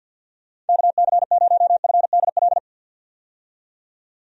Morse code, heard as a keyed tone, about 47 wpm, 700 Hz. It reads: KC0FDL
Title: USA Call Signs - 6 Characters - 50wpm